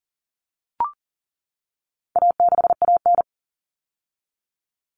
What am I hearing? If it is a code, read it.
Morse code, heard as a keyed tone, about 40 wpm, 700 Hz. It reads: A6AN
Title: Common Cut Numbers - 4-Digit 40wpm